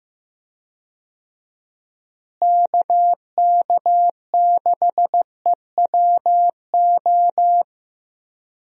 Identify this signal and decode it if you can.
Morse code, heard as a keyed tone, about 15 wpm, 700 Hz. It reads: KK6EWO
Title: USA Call Signs - 6 Characters - Repeated 2x - 15wpm